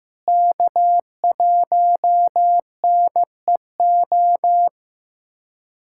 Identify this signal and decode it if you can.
Morse code, heard as a keyed tone, about 15 wpm, 700 Hz. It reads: K1NEO
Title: USA Callsign DE Callsign - 5 Characters - 15wpm